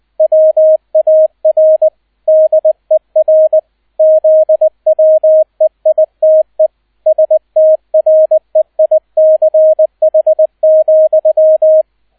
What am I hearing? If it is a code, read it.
Morse code, heard as a keyed tone, about 20 wpm, 630 Hz, with radio noise. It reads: WAR DER ZWEITE STREICH,